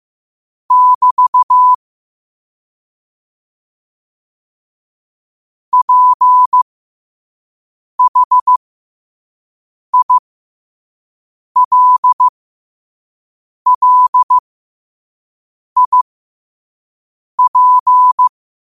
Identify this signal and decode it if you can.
Morse code, heard as a keyed tone, about 15 wpm, spaced as for 5 wpm, 1000 Hz. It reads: <BT> PHILLIP
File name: qotdfiles / 5wpm